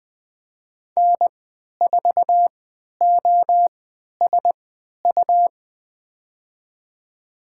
Morse code, heard as a keyed tone, about 20 wpm, spaced as for 11 wpm, 700 Hz. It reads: N4OSU
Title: USA Call Signs - 5 Characters - 3x Character Spacing - 20wpm